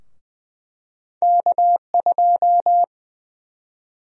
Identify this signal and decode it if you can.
Morse code, heard as a keyed tone, about 20 wpm, 700 Hz. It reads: K2
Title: USA Call Sign Prefixes - Repeated 2x - 20wpm